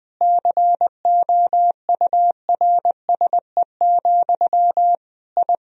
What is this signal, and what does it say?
Morse code, heard as a keyed tone, about 20 wpm, 700 Hz. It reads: COURSE, I
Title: Sentences from Top 1000 Words - Review Part 2 - 20wpm